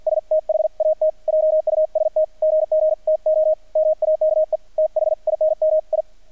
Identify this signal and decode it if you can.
Morse code, heard as a keyed tone, about 48 wpm, 630 Hz, with radio noise. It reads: S THAT JUST GG TO MAKE THINM I
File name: archived / CRSnet